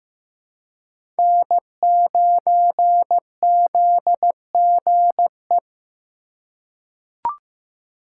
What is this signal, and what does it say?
Morse code, heard as a keyed tone, about 15 wpm, 700 Hz. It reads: N9ZGE
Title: USA Call Signs - 5 Characters - 3x Character Spacing - 15wpm